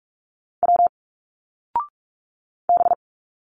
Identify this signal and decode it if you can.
Morse code, heard as a keyed tone, about 46 wpm, 700 Hz. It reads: W B
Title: Single Letters 45wpm